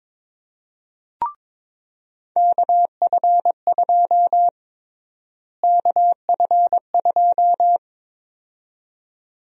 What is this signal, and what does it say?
Morse code, heard as a keyed tone, about 22 wpm, 700 Hz. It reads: KF2 KF2
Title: USA Call Sign Prefixes - Repeated 2x - 22wpm